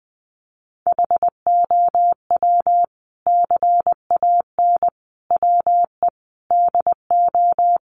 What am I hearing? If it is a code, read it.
Morse code, heard as a keyed tone, about 20 wpm, 700 Hz. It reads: HOW CAN WE DO
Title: Sentences from Top 600 Words 20wpm